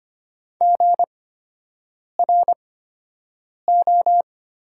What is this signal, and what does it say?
Morse code, heard as a keyed tone, about 25 wpm, spaced as for 7 wpm, 700 Hz. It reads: GRO
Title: 8x Character Spacing - Top 500 Words - 25wpm